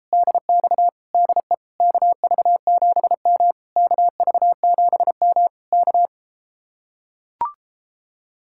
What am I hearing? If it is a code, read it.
Morse code, heard as a keyed tone, about 33 wpm, 700 Hz. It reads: DX DE KV7M KV7M K